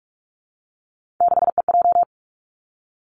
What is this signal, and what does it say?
Morse code, heard as a keyed tone, about 46 wpm, 700 Hz. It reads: 6EJ
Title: Three Challenging Letter-Number Combinations 45wpm